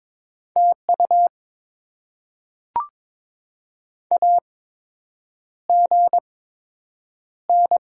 Morse code, heard as a keyed tone, about 22 wpm, spaced as for 8 wpm, 700 Hz. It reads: TU AGN
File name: Top 20 QSO Elements - 8x Character Spacing - 22wpm